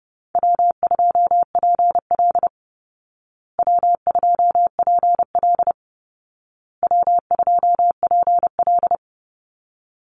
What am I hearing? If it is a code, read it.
Morse code, heard as a keyed tone, about 30 wpm, 700 Hz. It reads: W2PL W2PL W2PL
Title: Easy Call Signs Repeated 3x - 30wpm